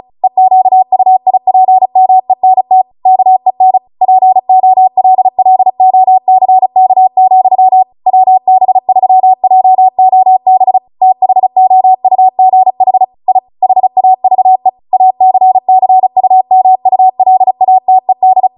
Morse code, heard as a keyed tone, about 36 wpm, 750 Hz. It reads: EQUIPMENT KEN POLLOCK, WB3JOB THOUGH I HAVE ACCUMULATED